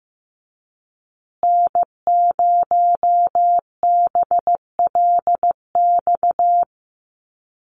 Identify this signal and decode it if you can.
Morse code, heard as a keyed tone, about 15 wpm, 700 Hz. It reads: N0BLX
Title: USA Call Signs Repeated 2x 15wpm